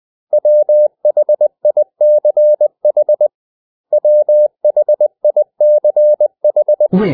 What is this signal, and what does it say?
Morse code, heard as a keyed tone, about 20 wpm, 600 Hz. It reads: WHICH WHICH
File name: cw / 100 Common English Words